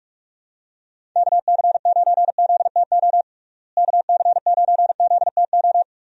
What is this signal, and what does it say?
Morse code, heard as a keyed tone, about 41 wpm, 700 Hz. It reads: KC9ZTO KC9ZTO
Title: USA Call Signs - 6 Characters - Repeated 3x - 45wpm